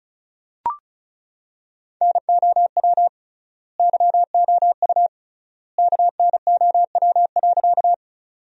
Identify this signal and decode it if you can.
Morse code, heard as a keyed tone, about 34 wpm, 700 Hz. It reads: NOW YOU KNOW.
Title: 3x Word-Spacing - Sentences from Top 100 Words - 35wpm